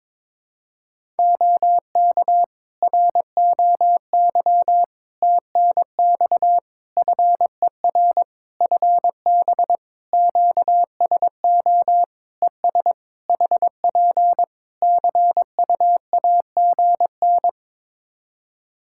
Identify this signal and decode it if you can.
Morse code, heard as a keyed tone, about 22 wpm, 700 Hz. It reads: OK ROY TNX FER FB QSO ES HP CUAGN
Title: Practice QSO - Exchange 4 - Ending - 22wpm